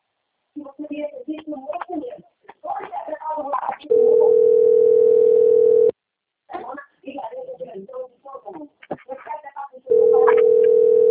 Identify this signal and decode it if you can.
Ringback tone (North America), 440 + 480 Hz, with speech.
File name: rg-1600-unknown- - - .70316